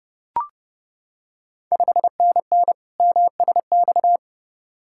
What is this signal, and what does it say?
Morse code, heard as a keyed tone, about 30 wpm, 690 Hz. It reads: 5NN MSX
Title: Ontario QSO Party Exchange 30wpm